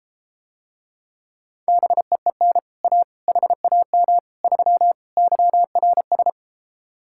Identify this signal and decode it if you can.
Morse code, heard as a keyed tone, about 33 wpm, 700 Hz. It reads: BEEN A HAM 3 YRS